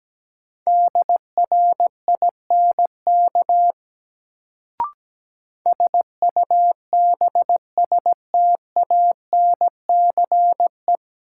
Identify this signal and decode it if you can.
Morse code, heard as a keyed tone, about 17 wpm, 700 Hz. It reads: DRINK SUBSTANCE
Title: Sentences from Top 1500 Words - New Word Drill Alt - 17wpm